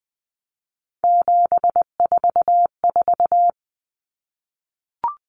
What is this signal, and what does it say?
Morse code, heard as a keyed tone, about 20 wpm, 700 Hz. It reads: 744
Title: Numbers - 3-Digit 20wpm